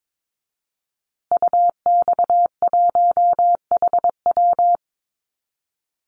Morse code, heard as a keyed tone, about 22 wpm, 700 Hz. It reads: UX1HW
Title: Call Signs Repeated 3x - 22wpm